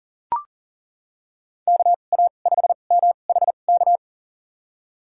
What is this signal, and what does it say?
Morse code, heard as a keyed tone, about 40 wpm, spaced as for 29 wpm, 700 Hz. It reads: KA5MHK
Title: USA Call Signs - 6 Characters - 2x Character Spacing - 40wpm